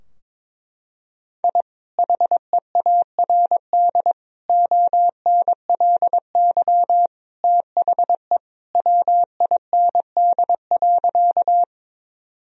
Morse code, heard as a keyed tone, about 22 wpm, 700 Hz. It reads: I HEARD ONLY THE WIND.
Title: Sentences from Top 1000 Words - Review Part 2 - 22wpm